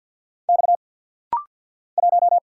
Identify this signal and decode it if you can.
Morse code, heard as a keyed tone, about 49 wpm, 700 Hz. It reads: X 1
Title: Single Letter-Number Alt - Rapid-Fire 50wpm